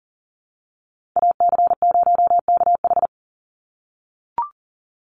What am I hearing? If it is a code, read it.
Morse code, heard as a keyed tone, about 40 wpm, 700 Hz. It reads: AC0KH
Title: USA Call Signs 40wpm